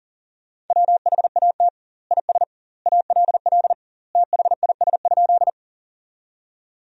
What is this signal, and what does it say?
Morse code, heard as a keyed tone, about 40 wpm, 700 Hz. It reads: WHAT IS ALL THIS?
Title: 2x Word-Spacing - Sentences from Top 100 Words - 40wpm